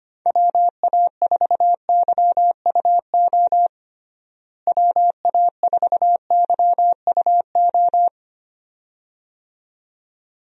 Morse code, heard as a keyed tone, about 25 wpm, 700 Hz. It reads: WA4YUO WA4YUO
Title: USA Call Signs - 6 Characters - Repeated 2x - 25wpm